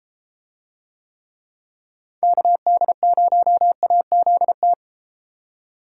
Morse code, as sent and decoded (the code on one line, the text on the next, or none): -.- -.. ----- .- --.. -
KD0AZT